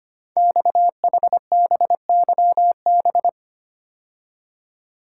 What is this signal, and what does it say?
Morse code, heard as a keyed tone, about 25 wpm, 700 Hz. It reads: XHBYB
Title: Code Groups - 5 Letters - 25wpm